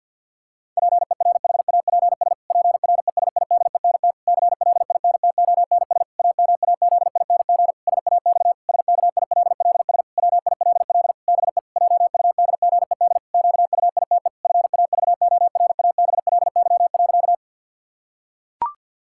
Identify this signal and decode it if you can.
Morse code, heard as a keyed tone, about 47 wpm, 700 Hz. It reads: PERHAPS PRESIDENT CLINTONS AMAZING SAX SKILLS WILL BE JUDGED QUITE FAVORABLY.
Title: Pangram Challenge v2 - 50wpm